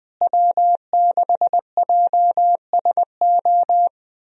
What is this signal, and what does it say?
Morse code, heard as a keyed tone, about 20 wpm, 700 Hz. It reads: W6JSO
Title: USA Call Signs - 5 Characters - Repeated 3x - 20wpm